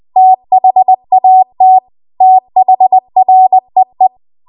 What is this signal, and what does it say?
Morse code, heard as a keyed tone, about 20 wpm, 750 Hz. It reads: THAT THREE